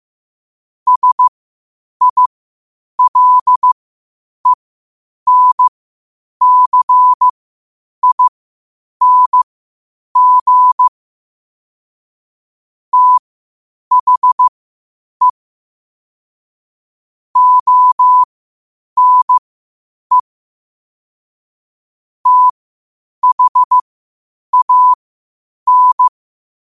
Morse code, heard as a keyed tone, about 15 wpm, spaced as for 8 wpm, 1000 Hz. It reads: SILENCING THE ONE THAN
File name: qotdfiles / 8wpm